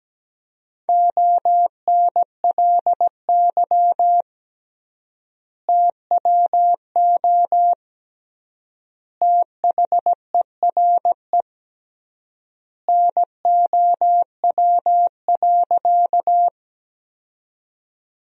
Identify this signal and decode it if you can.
Morse code, heard as a keyed tone, about 17 wpm, 700 Hz. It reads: ONLY TWO THERE NOW.
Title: 3x Word-Spacing - Sentences from Top 200 Words - 17wpm